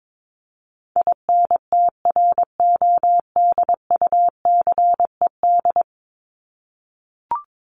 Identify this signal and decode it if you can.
Morse code, heard as a keyed tone, about 22 wpm, 700 Hz. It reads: INTRODUCED
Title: ED Suffix - Top 100 Words with Base Words 22wpm